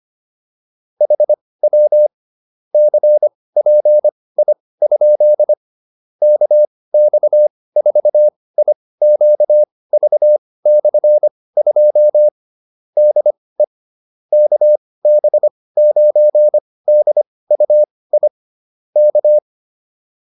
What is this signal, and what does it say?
Morse code, heard as a keyed tone, about 25 wpm, spaced as for 18 wpm, 600 Hz. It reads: HW CPI? KX4IQV/2 DE KB9DUI K
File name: QSO308 18